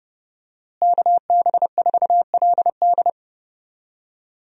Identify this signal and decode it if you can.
Morse code, heard as a keyed tone, about 32 wpm, 700 Hz. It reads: KB4LD
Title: USA Call Signs - 5 Characters - Repeated 2x - 30wpm